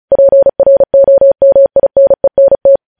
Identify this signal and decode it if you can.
Morse code, heard as a keyed tone, about 36 wpm, 550 Hz. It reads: PROMINENT
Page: News headlines in Morse code at 35 wpm